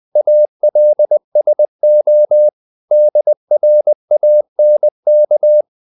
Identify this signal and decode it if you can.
Morse code, heard as a keyed tone, about 20 wpm, 600 Hz. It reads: ALSO DRANK
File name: the old man and the sea / the old man and the sea part 1 20wpm